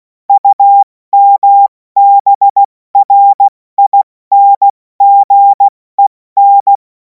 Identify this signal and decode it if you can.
Morse code, heard as a keyed tone, about 16 wpm, 800 Hz. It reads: UMBRINGEN
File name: Morsekurs / Witze / Witz 11. August 2026 70 F80